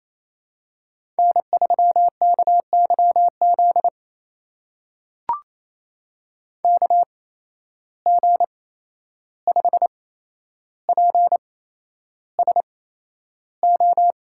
Morse code, heard as a keyed tone, about 28 wpm, spaced as for 11 wpm, 700 Hz. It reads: N3KYZ KG5PSO
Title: USA Call Signs - 8x Character Spacing - 28wpm